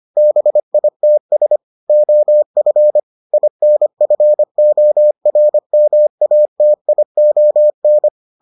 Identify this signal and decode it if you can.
Morse code, heard as a keyed tone, about 25 wpm, 600 Hz. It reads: BITS OF INFORMATION